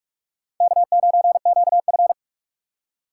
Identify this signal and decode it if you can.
Morse code, heard as a keyed tone, about 46 wpm, 700 Hz. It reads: K9QF
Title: USA Call Signs - 4 Characters - 2x Character Spacing - 45wpm